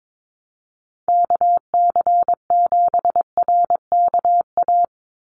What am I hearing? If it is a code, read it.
Morse code, heard as a keyed tone, about 22 wpm, 700 Hz. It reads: KC7RKA
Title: USA Call Signs 22wpm